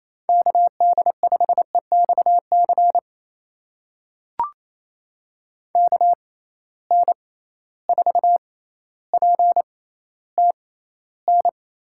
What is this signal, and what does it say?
Morse code, heard as a keyed tone, about 28 wpm, spaced as for 14 wpm, 700 Hz. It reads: KD5EXC KN4PTN
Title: USA Call Signs - 6 Characters - 6x Character Spacing - 28wpm